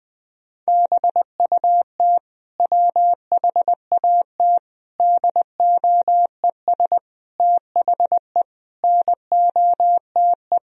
Morse code, heard as a keyed tone, about 19 wpm, 700 Hz. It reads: BUT WHAT DOES THE NOTE
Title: Sentences from Top 1000 Words - Review Part 2 - 20wpm